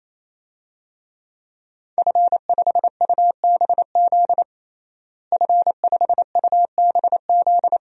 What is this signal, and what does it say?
Morse code, heard as a keyed tone, about 27 wpm, 700 Hz. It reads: F5UBZ F5UBZ
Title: DXCC-100 Call Signs - 5 Characters - Repeated 3x - 28wpm